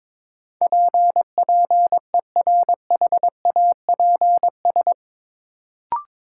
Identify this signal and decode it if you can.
Morse code, heard as a keyed tone, about 21 wpm, 700 Hz. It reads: PPERHAPS
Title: Sentences from Top 1100 Words - New Word Drill Alt - 22wpm